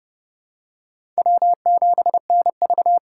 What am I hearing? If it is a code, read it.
Morse code, heard as a keyed tone, about 29 wpm, 700 Hz. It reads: W7NV
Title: USA Call Signs - 3x Character Spacing - 30wpm